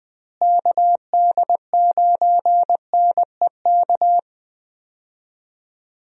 Morse code, heard as a keyed tone, about 19 wpm, 700 Hz. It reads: KD9NEK
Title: USA Call Signs - 6 Characters - Repeated 3x - 20wpm